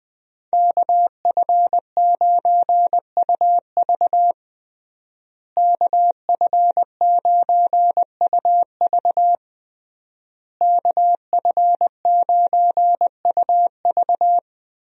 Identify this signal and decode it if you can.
Morse code, heard as a keyed tone, about 20 wpm, 700 Hz. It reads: KF9UV KF9UV KF9UV
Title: USA Call Signs - 5 Characters - Repeated 3x - 20wpm